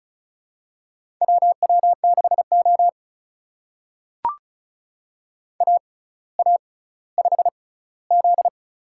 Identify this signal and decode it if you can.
Morse code, heard as a keyed tone, about 36 wpm, spaced as for 18 wpm, 700 Hz. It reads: WW6O AA5Z
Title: USA Call Signs - 4 Characters - 6x Character Spacing - 35wpm